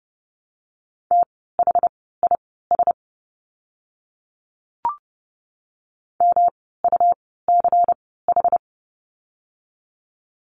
Morse code, heard as a keyed tone, about 30 wpm, spaced as for 17 wpm, 700 Hz. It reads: THIS MUCH
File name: course / 30:10wpm / 108 Lesson 18f - Letter m - Words 30wpm